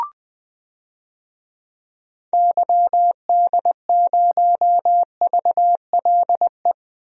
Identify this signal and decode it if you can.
Morse code, heard as a keyed tone, about 20 wpm, 700 Hz. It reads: YD0VLE
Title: DXCC-100 Call Signs - 6 Characters - Repeated 3x - 20wpm